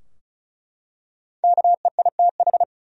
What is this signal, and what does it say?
Morse code, heard as a keyed tone, about 34 wpm, 700 Hz. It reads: KEITH
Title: USA Common Names - Max 5 Letters - 35wpm